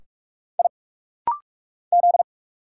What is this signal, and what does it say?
Morse code, heard as a keyed tone, about 46 wpm, 700 Hz. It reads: I Z
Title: Single Letters - Rapid-Fire 45wpm